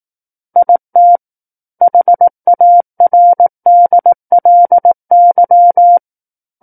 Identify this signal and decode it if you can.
Morse code, heard as a keyed tone, about 18 wpm, 700 Hz. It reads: IT HARDLY
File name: 18 WPM